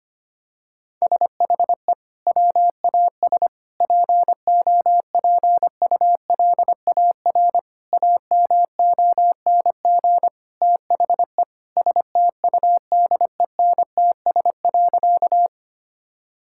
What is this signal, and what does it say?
Morse code, heard as a keyed tone, about 25 wpm, 700 Hz. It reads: SHE WAS POPULAR AMONG THE STUDENTS.
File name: practice / 25wpm / Sentences with 2-Word Phrases - 25wpm